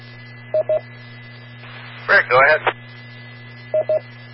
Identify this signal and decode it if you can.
Speech. There is no code.